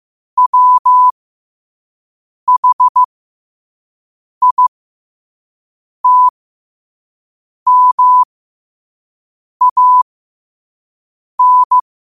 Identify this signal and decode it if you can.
Morse code, heard as a keyed tone, about 15 wpm, spaced as for 5 wpm, 1000 Hz. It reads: WHITMAN
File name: qotdfiles / 5wpm